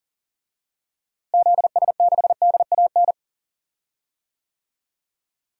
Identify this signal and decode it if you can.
Morse code, heard as a keyed tone, about 40 wpm, 700 Hz. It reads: ZS6DAN
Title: DXCC-100 Call Signs - 6 Characters - Repeated 3x - 40wpm